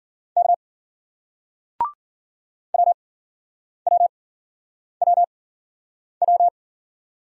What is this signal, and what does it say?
Morse code, heard as a keyed tone, about 53 wpm, spaced as for 9 wpm, 700 Hz. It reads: K WWWW